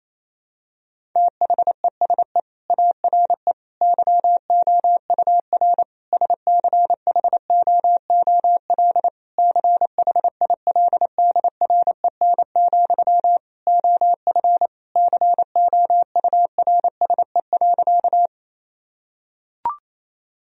Morse code, heard as a keyed tone, about 28 wpm, 700 Hz. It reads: THESE ARE YOUR SCHOOL CHILDREN, OF COURSE.
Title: Sentences from Top 1000 Words - Review Part 1 - 28wpm